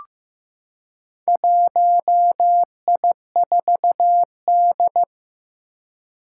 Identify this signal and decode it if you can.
Morse code, heard as a keyed tone, about 15 wpm, 700 Hz. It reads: 1I4D
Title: Four Challenging Letter-Number Combinations 15wpm